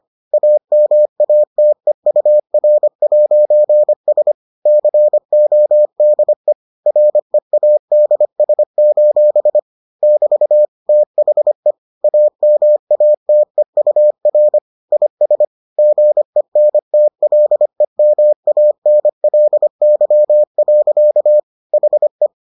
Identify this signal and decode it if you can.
Morse code, heard as a keyed tone, about 25 wpm, 600 Hz. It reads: AMATEUR'S CODE READS: <BT> THE AMATEUR IS GENTLEMANLY. HE